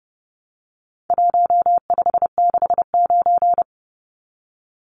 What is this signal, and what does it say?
Morse code, heard as a keyed tone, about 29 wpm, 700 Hz. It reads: 1569
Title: Numbers - Contest Style - 4-Digit 30wpm